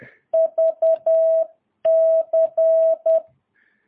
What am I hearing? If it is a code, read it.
Morse code, heard as a keyed tone, about 10 wpm, 650 Hz. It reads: VC
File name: jh8cbh / 2022cw